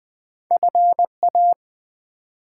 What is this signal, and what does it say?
Morse code, heard as a keyed tone, about 20 wpm, 700 Hz. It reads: FA
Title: Build A Word - 1 Letter At A Time - 4-Letter Words - 20wpm